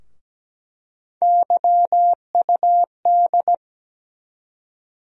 Morse code, heard as a keyed tone, about 17 wpm, 700 Hz. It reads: YUD
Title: USA Call Sign Postfixes - 3 Letters 17wpm